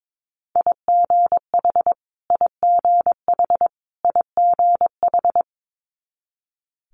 Morse code, heard as a keyed tone, about 22 wpm, 700 Hz. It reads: IGH IGH IGH
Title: Multi-letter Phonemes Repeated 3x 22wpm